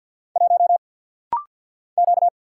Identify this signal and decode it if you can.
Morse code, heard as a keyed tone, about 50 wpm, 690 Hz. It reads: 1 Q
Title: Single Letter-Number - Rapid-Fire 50wpm